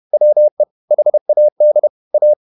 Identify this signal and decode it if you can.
Morse code, heard as a keyed tone, about 31 wpm, 600 Hz. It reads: WE HAD A